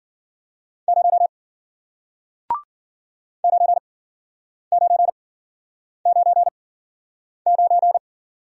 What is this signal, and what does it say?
Morse code, heard as a keyed tone, about 54 wpm, 700 Hz. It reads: 0 9 9 9 9